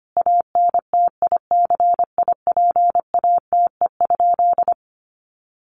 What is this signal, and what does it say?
Morse code, heard as a keyed tone, about 25 wpm, 700 Hz. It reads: ANTICIPATE?
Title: Sentences from Top 1500 Words 25wpm